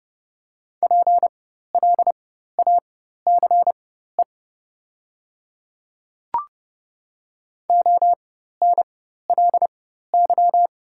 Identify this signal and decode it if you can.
Morse code, heard as a keyed tone, about 30 wpm, spaced as for 14 wpm, 700 Hz. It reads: PLACE ONLY
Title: 4x Character Spacing - Top 200 Words - 30wpm